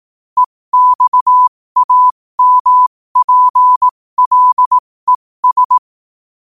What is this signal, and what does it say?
Morse code, heard as a keyed tone, about 18 wpm, spaced as for 15 wpm, 1000 Hz. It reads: EXAMPLES
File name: qotdfiles / 15wpm